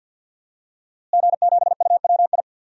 Morse code, heard as a keyed tone, about 46 wpm, 700 Hz. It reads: G7UWI
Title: DXCC-100 Call Signs - 5 Characters - Repeated 3x - 50wpm